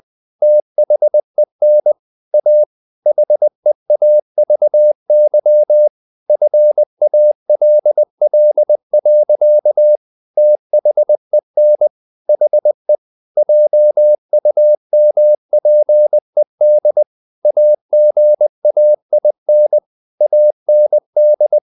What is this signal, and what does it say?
Morse code, heard as a keyed tone, about 20 wpm, 600 Hz. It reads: THEN A HEAVY FALL. THEN HE JUMPED AGAIN AND
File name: the old man and the sea part 2 20wpm